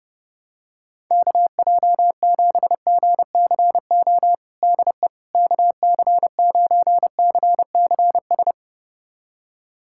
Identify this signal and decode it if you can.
Morse code, heard as a keyed tone, about 30 wpm, 700 Hz. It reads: KJ7GCO DE KC9CCS
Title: USA Callsign DE Callsign 30wpm